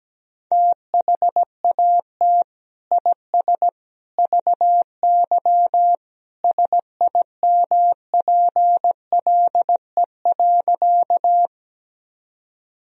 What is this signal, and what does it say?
Morse code, heard as a keyed tone, about 17 wpm, 700 Hz. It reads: THAT IS VY SIMPLE.
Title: Sentences from Top 400 Words and Abbreviations 17wpm